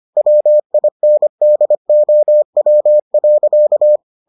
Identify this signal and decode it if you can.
Morse code, heard as a keyed tone, about 25 wpm, 600 Hz. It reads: WINDOW.